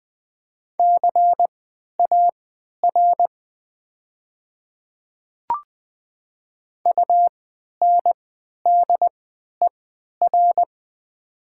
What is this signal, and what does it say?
Morse code, heard as a keyed tone, about 20 wpm, spaced as for 11 wpm, 700 Hz. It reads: CAR UNDER
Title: 3x Character Spacing - Top 300 Words - 20wpm